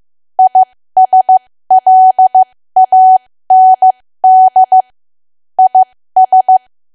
Morse code, heard as a keyed tone, about 15 wpm, spaced as for 13 wpm, 750 Hz. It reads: ISLAND IS
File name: cwa / W1AW-Level2 / 13WPM